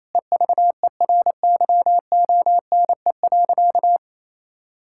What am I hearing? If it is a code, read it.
Morse code, heard as a keyed tone, about 28 wpm, 700 Hz. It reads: EVERYONE.
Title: Sentences from Top 1000 Words - Review Part 5 - 28wpm